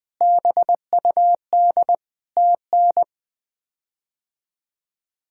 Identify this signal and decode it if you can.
Morse code, heard as a keyed tone, about 21 wpm, 700 Hz. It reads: BUD TN
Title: K1USN SST QSO Practice - 20wpm